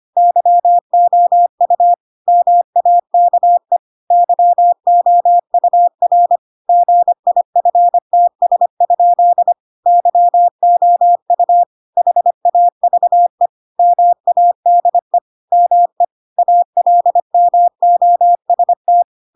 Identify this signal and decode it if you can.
Morse code, heard as a keyed tone, about 25 wpm, 700 Hz. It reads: YOU MAKE YOUR GIFTS? YOU HAVE MADE ME ALMOST
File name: Pinocchio / Pinocchio ch2 25 25 ik1hge